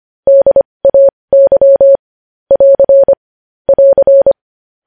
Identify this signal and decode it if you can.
Morse code, heard as a keyed tone, about 25 wpm, spaced as for 20 wpm, 550 Hz. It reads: DAY <AR> <AR>
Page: news-headlines-in-morse-code-at-20-wpm